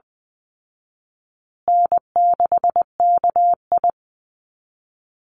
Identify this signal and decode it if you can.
Morse code, heard as a keyed tone, about 20 wpm, 700 Hz. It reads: N6KI
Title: Easy Call Signs Repeated 3x - 20wpm